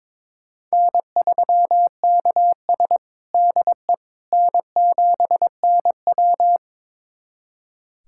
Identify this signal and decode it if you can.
Morse code, heard as a keyed tone, about 22 wpm, 700 Hz. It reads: N3KS DE N7NW
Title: USA Callsign DE Callsign - 4 Characters - 22wpm